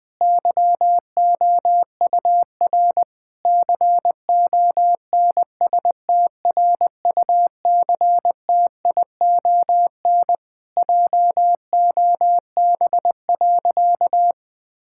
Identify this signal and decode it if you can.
Morse code, heard as a keyed tone, about 20 wpm, 700 Hz. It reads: YOUR CONSTRUCTION JOB.
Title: Sentences from Top 1600 Words 20wpm